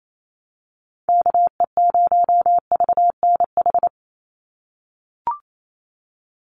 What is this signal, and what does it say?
Morse code, heard as a keyed tone, about 28 wpm, 700 Hz. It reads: KE0VNH
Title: USA Call Signs 28wpm